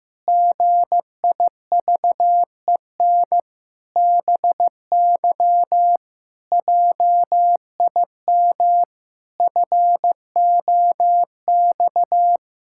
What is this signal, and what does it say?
Morse code, heard as a keyed tone, about 15 wpm, 700 Hz. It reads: GIVEN BY JIM FOX
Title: Pangram Challenge v2 - 15wpm